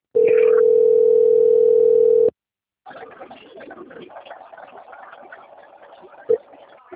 Ringback tone (North America), 440 + 480 Hz.